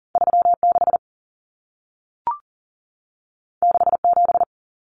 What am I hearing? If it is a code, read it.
Morse code, heard as a keyed tone, about 40 wpm, 700 Hz. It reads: 36 67
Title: Numbers - 2-Digit 40wpm